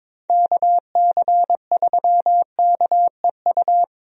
Morse code, heard as a keyed tone, about 21 wpm, 700 Hz. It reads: KC3KEU
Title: USA Call Signs - 6 Characters - Repeated 3x - 22wpm